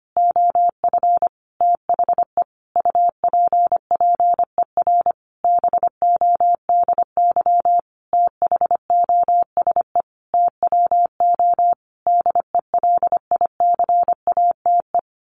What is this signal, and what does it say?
Morse code, heard as a keyed tone, about 25 wpm, 700 Hz. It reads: OF THE UPPER BODY THOSE TWO DELICATE
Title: A Sound Of Thunder 25wpm